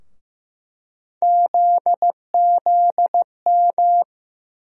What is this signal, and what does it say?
Morse code, heard as a keyed tone, about 15 wpm, 700 Hz. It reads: ZZM
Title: USA Call Sign Postfixes - 3 Letters 15wpm